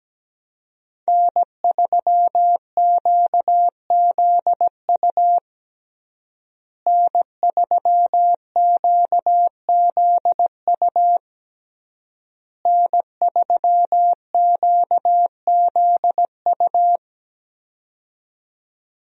Morse code, heard as a keyed tone, about 17 wpm, 700 Hz. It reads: N3QZU N3QZU N3QZU
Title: USA Call Signs - 5 Characters - Repeated 3x - 17wpm